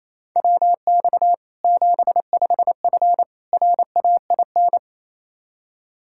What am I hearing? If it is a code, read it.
Morse code, heard as a keyed tone, about 27 wpm, 700 Hz. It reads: WX 75F RAIN
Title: Practice QSO - Exchange 2 - Info - 28wpm